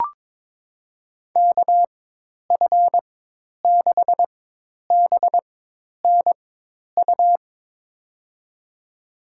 Morse code, heard as a keyed tone, about 21 wpm, spaced as for 10 wpm, 700 Hz. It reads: KF6BNU
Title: USA Call Signs - 6 Characters - 4x Character Spacing - 22wpm